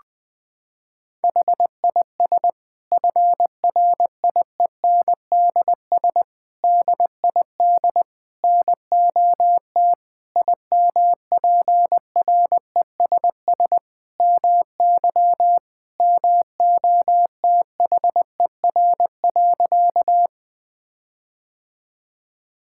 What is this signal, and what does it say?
Morse code, heard as a keyed tone, about 20 wpm, 700 Hz. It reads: HIS FRIENDS DID NOT IMPRESS MY MOTHER.
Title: Sentences from Top 1600 Words 20wpm